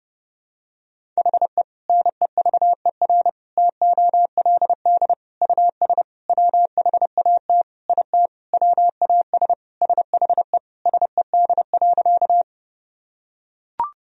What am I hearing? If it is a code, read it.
Morse code, heard as a keyed tone, about 32 wpm, 700 Hz. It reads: HE NEVER TOLD US WHAT IT WAS SHE SED.
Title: Sentences from Top 400 Words and Abbreviations 30wpm